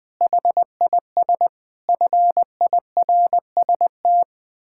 Morse code, heard as a keyed tone, about 21 wpm, 700 Hz. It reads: HIS FIRST
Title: Pangram Challenge v2 - 20wpm